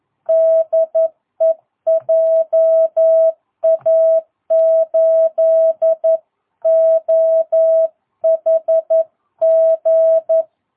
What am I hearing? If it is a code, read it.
Morse code, heard as a keyed tone, about 11 wpm, 660 Hz. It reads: DE JA8OHG
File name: jh8cbh / 2022cw